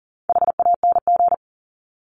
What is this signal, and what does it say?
Morse code, heard as a keyed tone, about 40 wpm, 700 Hz. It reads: HANG
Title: Top 1501-1750 Words 40wpm